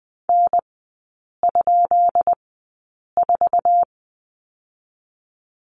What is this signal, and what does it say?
Morse code, heard as a keyed tone, about 19 wpm, spaced as for 8 wpm, 700 Hz. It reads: N?4
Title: 205 Lesson 33e - Letter j - Three Characters 20wpm